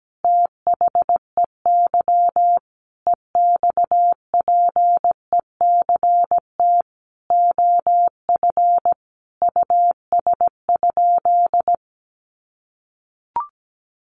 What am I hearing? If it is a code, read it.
Morse code, heard as a keyed tone, about 17 wpm, 700 Hz. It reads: THEY EXPECT OF US?
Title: Sentences from Top 1400 Words 17wpm